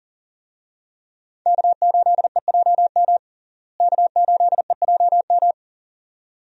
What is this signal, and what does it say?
Morse code, heard as a keyed tone, about 37 wpm, 700 Hz. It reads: K8EJM K8EJM
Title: USA Call Signs - 5 Characters - Repeated 2x - 40wpm